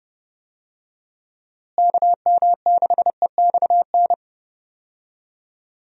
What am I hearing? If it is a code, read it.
Morse code, heard as a keyed tone, about 30 wpm, 700 Hz. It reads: KM6EXN
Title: USA Call Signs - 6 Characters - Repeated 2x - 30wpm